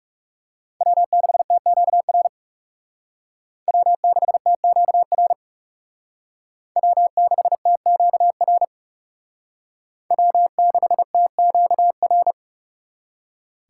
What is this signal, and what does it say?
Morse code, heard as a keyed tone, about 36 wpm, 700 Hz. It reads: W6TQR W6TQR W6TQR W6TQR
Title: Speed Racing - USA 1x3 Call Signs 30wpm